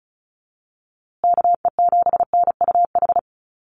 Morse code, heard as a keyed tone, about 34 wpm, 700 Hz. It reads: KE7NUH
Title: USA Call Signs Repeated 3x 35wpm